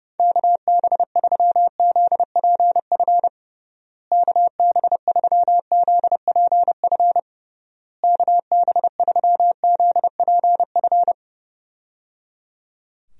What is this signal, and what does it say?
Morse code, heard as a keyed tone, about 32 wpm, 700 Hz. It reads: KB3ZPF KB3ZPF KB3ZPF
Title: USA Call Signs - 6 Characters - Repeated 3x - 30wpm